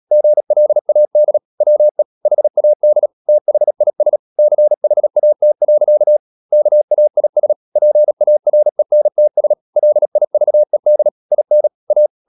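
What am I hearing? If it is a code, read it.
Morse code, heard as a keyed tone, about 37 wpm, 600 Hz. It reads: GLAD WE HAD THIS CHAT. KAIS PARENTS LIVED IN A
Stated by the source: Morse code practice at 35–39 wpm sending English text of Ken Follett's novel Never, from CW med Gnister.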